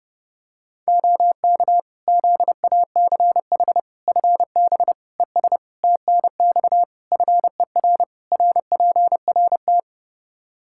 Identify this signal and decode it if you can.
Morse code, heard as a keyed tone, about 30 wpm, 700 Hz. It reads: OK ZACH FB ES TNX FER RPRT
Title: Practice QSO - Exchange 2 - Info - 30wpm